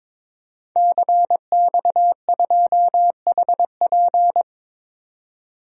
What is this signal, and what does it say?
Morse code, heard as a keyed tone, about 22 wpm, 700 Hz. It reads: CX2HP
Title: DXCC-100 Call Signs - 5 Characters - 22wpm